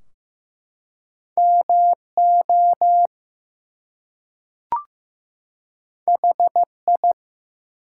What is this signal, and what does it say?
Morse code, heard as a keyed tone, about 15 wpm, 700 Hz. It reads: MO HI
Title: US States and Canadian Provinces 15wpm